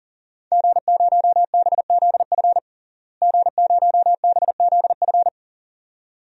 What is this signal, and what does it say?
Morse code, heard as a keyed tone, about 40 wpm, 700 Hz. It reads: G0BZF G0BZF
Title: DXCC-100 Call Signs - 5 Characters - Repeated 2x - 40wpm